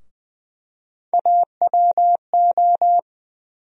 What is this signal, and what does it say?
Morse code, heard as a keyed tone, about 20 wpm, 700 Hz. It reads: AWO
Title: USA Call Sign Postfixes - 3 Letters 20wpm